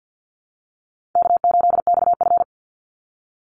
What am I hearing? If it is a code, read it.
Morse code, heard as a keyed tone, about 50 wpm, 700 Hz. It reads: K8XF
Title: Call Signs - 50wpm